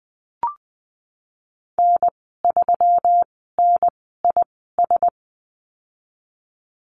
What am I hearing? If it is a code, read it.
Morse code, heard as a keyed tone, about 20 wpm, spaced as for 15 wpm, 700 Hz. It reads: N3NIS
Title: 186 Lesson 30g - Letter b - Call Signs 20wpm